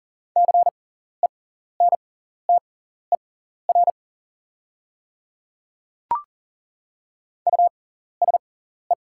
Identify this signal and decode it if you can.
Morse code, heard as a keyed tone, about 40 wpm, spaced as for 14 wpm, 690 Hz. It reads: CENTER USE
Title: 6x Character Spacing - Top 500 Words - 40wpm